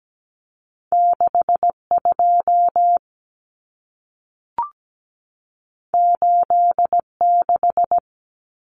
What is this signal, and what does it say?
Morse code, heard as a keyed tone, about 17 wpm, 700 Hz. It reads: 62 86
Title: Numbers - 2-Digit 17wpm